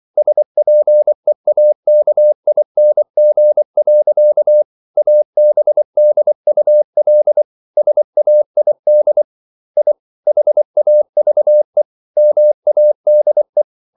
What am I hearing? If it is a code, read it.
Morse code, heard as a keyed tone, about 24 wpm, 600 Hz. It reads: SPEAKING. ABDUL SAID I HAVE MADE